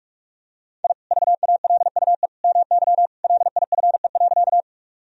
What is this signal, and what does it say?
Morse code, heard as a keyed tone, about 46 wpm, 700 Hz. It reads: I VALUE MY LIFE.
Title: Sentences from Top 1000 Words - Review Part 3 - 45wpm